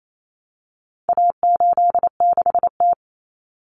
Morse code, heard as a keyed tone, about 28 wpm, 700 Hz. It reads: A86T
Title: Common Cut Numbers - 4-Digit 28wpm